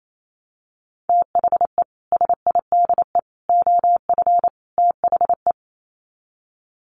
Morse code, heard as a keyed tone, about 28 wpm, 700 Hz. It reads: THE SIDE OF THE
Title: Sets of 4 Words 28wpm